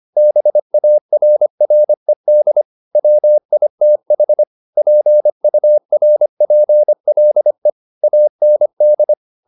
Morse code, heard as a keyed tone, about 25 wpm, 600 Hz. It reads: BARRED WITH PURPLE AND